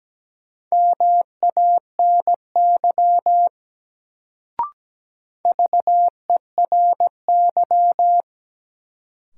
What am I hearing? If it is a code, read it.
Morse code, heard as a keyed tone, about 17 wpm, 700 Hz. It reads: MANY VERY
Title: Sets of 2 Words - New Word Drill - 17wpm